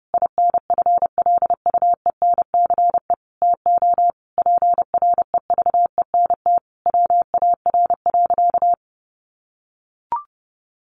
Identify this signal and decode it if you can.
Morse code, heard as a keyed tone, about 30 wpm, 700 Hz. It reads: INFLUENCE TO PREVENT WAR.
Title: Sentences from Top 1100 Words 30wpm